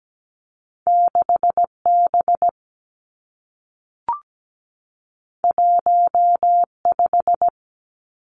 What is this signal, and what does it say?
Morse code, heard as a keyed tone, about 17 wpm, 700 Hz. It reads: 6 B 1 5
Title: Two Challenging Letter-Number Combinations 17wpm